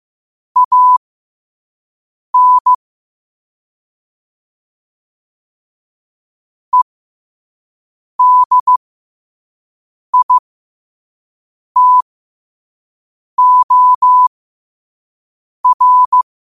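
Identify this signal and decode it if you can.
Morse code, heard as a keyed tone, about 15 wpm, spaced as for 5 wpm, 1000 Hz. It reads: AN EDITOR
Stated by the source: Morse practice at 5 wpm.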